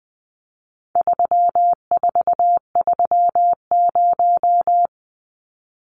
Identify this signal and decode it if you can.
Morse code, heard as a keyed tone, about 20 wpm, 700 Hz. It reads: 3430
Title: Numbers - 4-Digit 20wpm